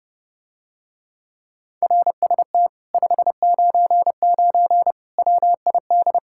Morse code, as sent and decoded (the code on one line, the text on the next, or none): .-. ... - / ..... ----. ----. / .-- .. -..
RST 599 WID